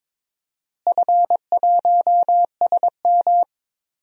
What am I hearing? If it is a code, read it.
Morse code, heard as a keyed tone, about 21 wpm, 700 Hz. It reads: F1SM
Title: DXCC-100 Call Signs - 4 Characters - Repeated 2x - 22wpm